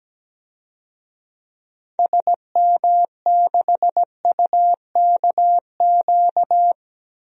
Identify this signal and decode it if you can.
Morse code, heard as a keyed tone, about 17 wpm, 700 Hz. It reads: SM6UKQ
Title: DXCC-100 Call Signs - 6 Characters - 17wpm